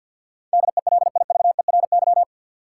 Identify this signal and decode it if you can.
Morse code, heard as a keyed tone, about 51 wpm, 700 Hz. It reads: DELIVERY
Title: Sentences from Top 1500 Words - New Word Drill - 50wpm